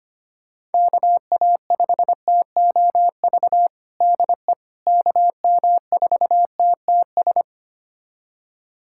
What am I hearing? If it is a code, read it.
Morse code, heard as a keyed tone, about 25 wpm, 700 Hz. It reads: KA5TOV DE KM4TTS
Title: USA Callsign DE Callsign - 6 Characters - 25wpm